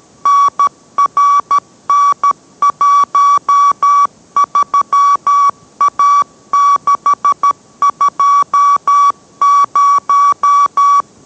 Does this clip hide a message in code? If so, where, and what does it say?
Yes: Morse code from 0.3 to 11.0 s, heard as a keyed tone, about 13 wpm, 1140 Hz. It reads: NRN13A620